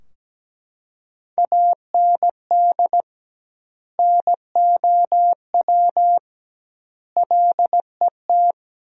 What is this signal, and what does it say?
Morse code, heard as a keyed tone, about 17 wpm, 700 Hz. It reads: AND NOW LET
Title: 2x Word-Spacing - Sentences from Top 300 Words - 17wpm